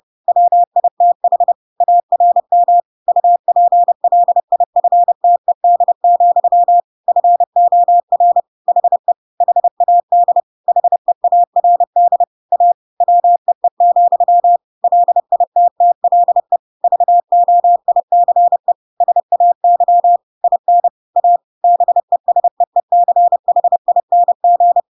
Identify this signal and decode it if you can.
Morse code, heard as a keyed tone, about 30 wpm, 700 Hz. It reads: WITH ARM UPLIFTED, FOR HE HAD HEARD A WEE, LITTLE VOICE SAY IN A BESEECHING
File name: Pinocchio / Pinocchio ch1 30 30 ik1hge